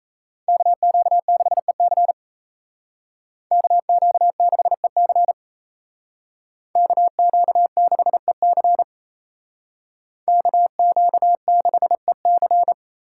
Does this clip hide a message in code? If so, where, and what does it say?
Yes: Morse code from 0.5 to 12.7 s, heard as a keyed tone, about 36 wpm, 700 Hz. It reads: KQ6EC KQ6EC KQ6EC KQ6EC